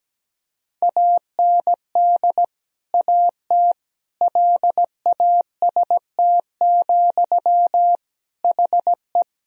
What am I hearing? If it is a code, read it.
Morse code, heard as a keyed tone, about 17 wpm, 700 Hz. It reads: AND AT LAST, HE
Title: Sentences from Top 1000 Words - Review Part 2 - 17wpm